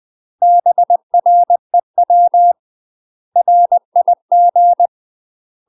Morse code, heard as a keyed tone, about 20 wpm, 700 Hz. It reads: BREW RIG
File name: UK MASTER QSOs TRAINING / 20WPM / UK MASTER QSOs 0061